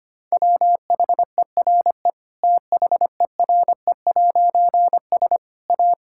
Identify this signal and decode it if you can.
Morse code, heard as a keyed tone, about 25 wpm, 700 Hz. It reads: WHERE THERE'S A
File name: practice / 25wpm / Difficult Sentences with Hints - 25wpm